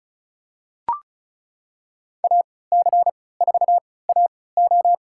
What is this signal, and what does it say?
Morse code, heard as a keyed tone, about 34 wpm, spaced as for 20 wpm, 700 Hz. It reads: AC4AO
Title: USA Call Signs - 5 Characters - 3x Character Spacing - 35wpm